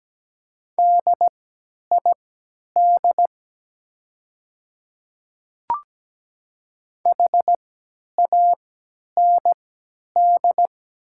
Morse code, heard as a keyed tone, about 17 wpm, spaced as for 10 wpm, 700 Hz. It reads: DID HAND
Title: 3x Character Spacing - Top 200 Words - 17wpm